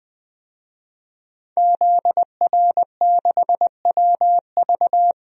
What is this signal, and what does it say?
Morse code, heard as a keyed tone, about 20 wpm, 700 Hz. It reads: ZR6WV
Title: DXCC-100 Call Signs - 5 Characters - 20wpm